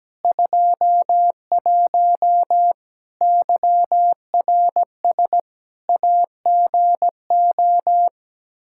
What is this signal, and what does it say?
Morse code, heard as a keyed tone, about 17 wpm, 700 Hz. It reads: 21 YRS AGO
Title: Practice QSO - Exchange 3 - More Info - 17wpm